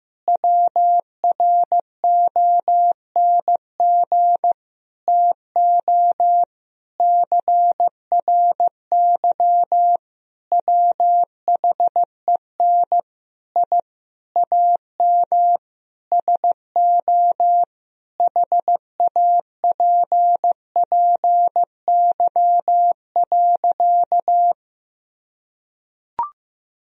Morse code, heard as a keyed tone, about 15 wpm, 700 Hz. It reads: WRONG TO CRY WHEN I AM SO HAPPY.
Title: Sentences from Top 1000 Words - Review Part 2 - 15wpm